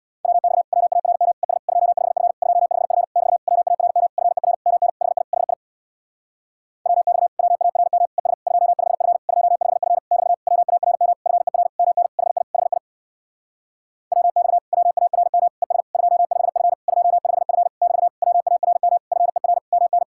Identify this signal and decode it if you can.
Morse code, heard as a keyed tone, about 64 wpm, 700 Hz. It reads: WX WARM ES 25F 25F <BT> WARM FER MN HI HI WX WARM ES 25F 25F <BT> WARM FER MN HI HI WX WARM ES 25F 25F <BT> WARM FER MN